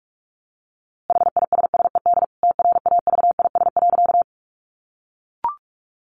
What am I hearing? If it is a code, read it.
Morse code, heard as a keyed tone, about 44 wpm, 700 Hz. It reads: HISSED TRAVIS.
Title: A Sound Of Thunder 45wpm